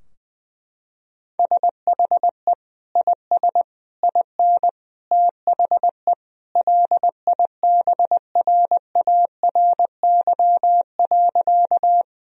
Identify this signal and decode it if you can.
Morse code, heard as a keyed tone, about 20 wpm, 700 Hz. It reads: SHE IS IN THE LIBRARY.
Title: Sentences from Top 1000 Words - Review Part 4 - 20wpm